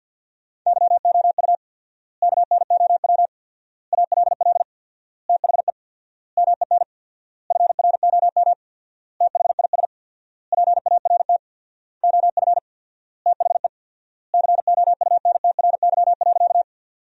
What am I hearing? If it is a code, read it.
Morse code, heard as a keyed tone, about 48 wpm, 700 Hz. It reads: YOU KNOW ALL THE MEN FROM THIS PART OF THE COUNTRY.
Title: 4x Word-Spacing - Sentences from Top 300 Words - 50wpm